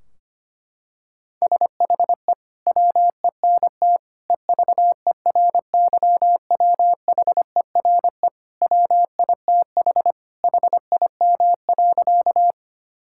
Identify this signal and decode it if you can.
Morse code, heard as a keyed tone, about 25 wpm, 700 Hz. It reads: SHE WENT EVERYWHERE WITH HIM.
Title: Sentences from Top 1600 Words 25wpm